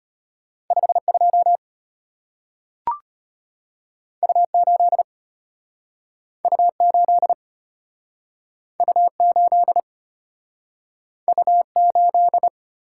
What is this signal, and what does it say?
Morse code, heard as a keyed tone, about 33 wpm, 690 Hz. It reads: 52 U8 U8 U8 U8